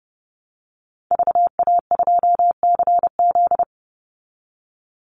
Morse code, heard as a keyed tone, about 30 wpm, 700 Hz. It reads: VA2CZ
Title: Call Signs Repeated 3x - 30wpm